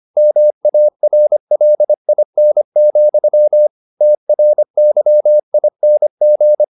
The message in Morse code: MARLIN, TRYING

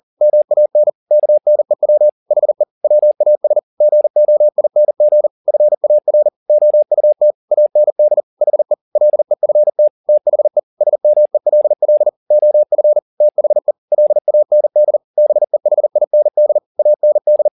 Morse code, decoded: MAN KNEW HE WAS GOING FAR OUT AND HE LEFT THE SMELL OF THE LAND BEHIND AND

